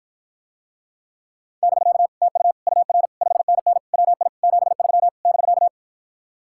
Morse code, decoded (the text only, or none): <BK> TU UR 5NN WI 73 <BK>